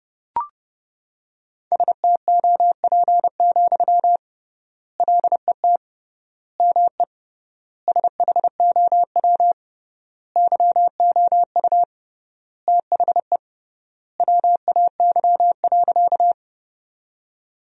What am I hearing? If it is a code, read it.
Morse code, heard as a keyed tone, about 32 wpm, 700 Hz. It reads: STOP, LET ME SHOW YOU THE WAY.
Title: 3x Word-Spacing - Sentences from Top 300 Words - 30wpm